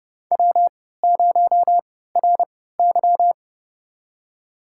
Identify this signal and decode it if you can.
Morse code, heard as a keyed tone, about 30 wpm, spaced as for 17 wpm, 700 Hz. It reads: W0RY
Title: USA Call Signs - 3x Character Spacing - 30wpm